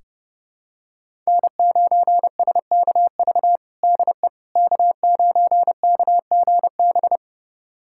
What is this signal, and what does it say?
Morse code, heard as a keyed tone, about 30 wpm, 700 Hz. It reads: N9SKV DE K9KGB
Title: USA Callsign DE Callsign - 5 Characters - 30wpm